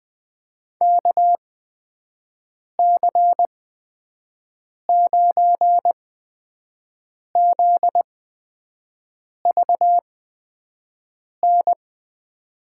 Morse code, heard as a keyed tone, about 20 wpm, spaced as for 5 wpm, 700 Hz. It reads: KC9ZVN
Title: USA Call Signs - 6 Characters - 8x Character Spacing - 20wpm